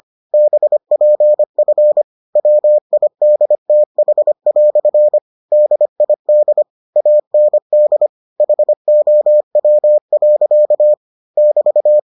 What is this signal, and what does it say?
Morse code, heard as a keyed tone, about 25 wpm, 600 Hz. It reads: BPF WIDTH" DID AND HOW. <BT>